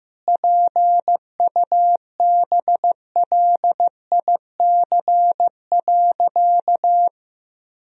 Morse code, heard as a keyed tone, about 16 wpm, 700 Hz. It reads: PUBLIC.
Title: Sentences from Top 1000 Words - Review Part 3 - 15wpm